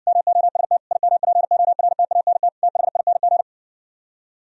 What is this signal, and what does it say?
Morse code, heard as a keyed tone, about 69 wpm, 700 Hz. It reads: MOST IMPORTANT THING